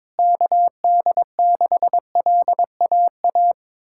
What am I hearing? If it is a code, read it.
Morse code, heard as a keyed tone, about 22 wpm, 700 Hz. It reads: KD6LAA
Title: USA Callsign DE Callsign - 6 Characters - 22wpm